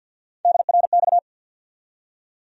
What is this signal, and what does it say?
Morse code, heard as a keyed tone, about 47 wpm, 700 Hz. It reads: DRX
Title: USA Call Sign Postfixes - 3 Letters 50wpm